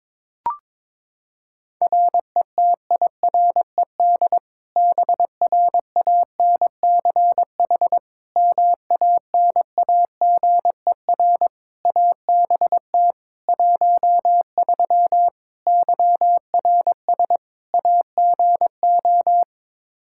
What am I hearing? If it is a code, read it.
Morse code, heard as a keyed tone, about 22 wpm, 700 Hz. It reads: RETIRED BRANCH MANAGER ABT 13 YRS AGO
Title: K1USN SST QSO Practice 22wpm